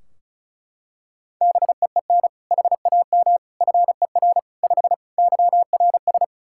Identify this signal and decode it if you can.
Morse code, heard as a keyed tone, about 36 wpm, 700 Hz. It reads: BEEN HAM FER 5 YRS
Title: Practice QSO - Exchange 3 - More Info - 35wpm